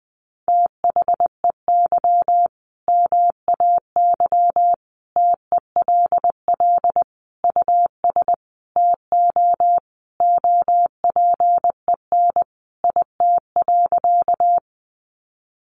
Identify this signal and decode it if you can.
Morse code, heard as a keyed tone, about 20 wpm, 700 Hz. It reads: THEY MAY TELL US TO OPEN IT.
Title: Sentences from Top 300 Words 20wpm